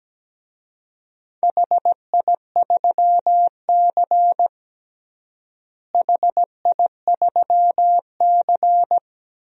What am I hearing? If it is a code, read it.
Morse code, heard as a keyed tone, about 17 wpm, 700 Hz. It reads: HI3C HI3C
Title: DXCC-100 Call Signs - 4 Characters - Repeated 2x - 17wpm